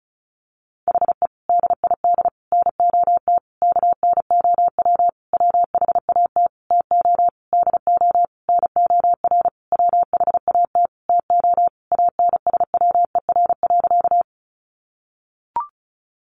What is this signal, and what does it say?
Morse code, heard as a keyed tone, about 34 wpm, 700 Hz. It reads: HE DID NOT KNOW WHAT TO DO NOR WHAT TO ANSWER.
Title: Sentences from Top 1500 Words 35wpm